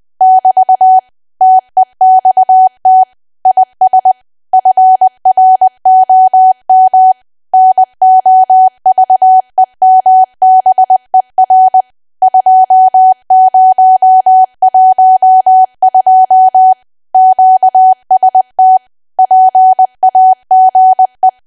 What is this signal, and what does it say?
Morse code, heard as a keyed tone, about 20 wpm, 750 Hz. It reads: <BT> TEXT IS FROM NOVEMBER 2012 QST PAGE
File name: cwa / W1AW-Level3 / 20WPM